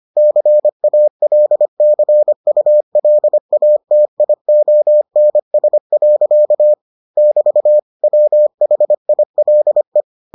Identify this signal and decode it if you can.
Morse code, heard as a keyed tone, about 25 wpm, 600 Hz. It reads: CALCULATIONS. <BT> WHILE